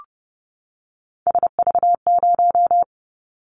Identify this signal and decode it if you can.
Morse code, heard as a keyed tone, about 30 wpm, 700 Hz. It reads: SV0
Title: Callsign Prefixes for Top 50 DXCC - 30wpm